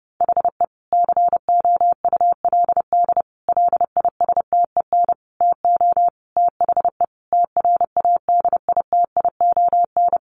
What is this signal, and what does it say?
Morse code, heard as a keyed tone, about 30 wpm, 700 Hz. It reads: HE COULD LISTEN TO THE TRADITION
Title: Sentences from Top 1500 Words 30wpm